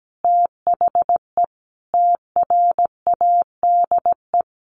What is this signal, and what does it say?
Morse code, heard as a keyed tone, about 17 wpm, 700 Hz. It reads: THE TRADE
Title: Sentences from Top 1500 Words 17wpm